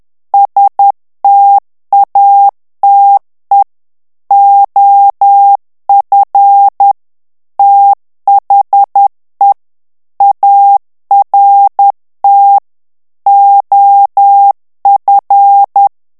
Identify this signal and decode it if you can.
Morse code, heard as a keyed tone, about 10 wpm, 800 Hz. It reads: STATE OF THE ART OF
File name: ANGLAIS texte clair 3 vitesse 10 mots minute